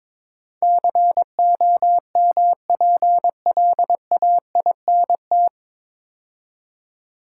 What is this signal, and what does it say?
Morse code, heard as a keyed tone, about 22 wpm, 700 Hz. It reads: COMPLAINT
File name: 9 Letter Words - First 3 Letters Given - 22wpm